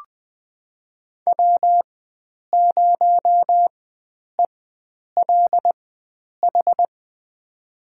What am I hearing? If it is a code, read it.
Morse code, heard as a keyed tone, about 20 wpm, spaced as for 9 wpm, 700 Hz. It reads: W0ELH
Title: USA Call Signs - 5 Characters - 4x Character Spacing - 20wpm